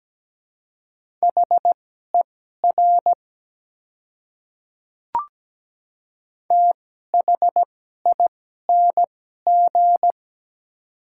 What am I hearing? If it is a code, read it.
Morse code, heard as a keyed tone, about 17 wpm, spaced as for 12 wpm, 700 Hz. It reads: HER THING
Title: 2x Character Spacing - Top 100 Words - 17wpm